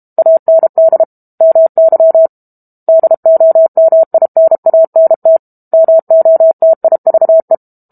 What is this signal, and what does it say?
Morse code, heard as a keyed tone, about 32 wpm, 650 Hz. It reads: AND MY DOMINANT MOTIVE
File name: Book Two / 06-Chapter 6 - 32 WPM